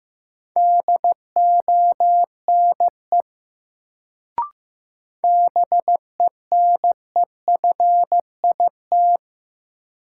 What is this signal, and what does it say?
Morse code, heard as a keyed tone, about 15 wpm, 700 Hz. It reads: DONE BENEFIT
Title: Sets of 5 Words Encore - New Word Drill - 15wpm